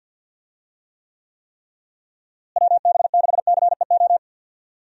Morse code, heard as a keyed tone, about 50 wpm, 700 Hz. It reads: WB6CEO
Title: USA Call Signs - 6 Characters - Repeated 3x - 50wpm